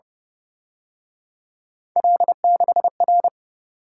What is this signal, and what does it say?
Morse code, heard as a keyed tone, about 30 wpm, 700 Hz. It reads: L6R